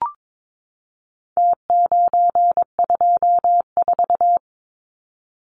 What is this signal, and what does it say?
Morse code, heard as a keyed tone, about 22 wpm, 700 Hz. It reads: T924
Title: Numbers - Contest Style - 4-Digit 22wpm